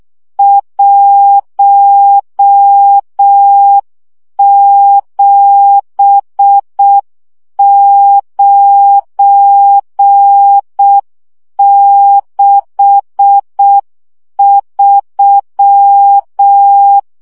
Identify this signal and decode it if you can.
Morse code, heard as a keyed tone, about 6 wpm, 800 Hz. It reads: 17963
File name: morze vizsga 2011-1 5